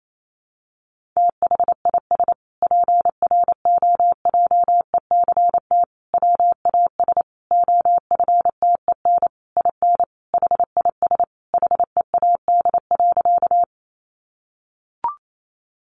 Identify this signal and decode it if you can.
Morse code, heard as a keyed tone, about 28 wpm, 700 Hz. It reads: THIS PROJECT WAS OFTEN IN HIS HEAD.
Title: Sentences from Top 1400 Words 28wpm